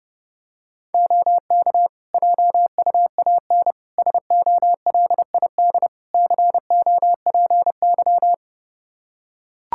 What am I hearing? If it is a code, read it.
Morse code, heard as a keyed tone, about 30 wpm, 700 Hz. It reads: OK JUAN SOLID COPY
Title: K1USN SST QSO Practice 30wpm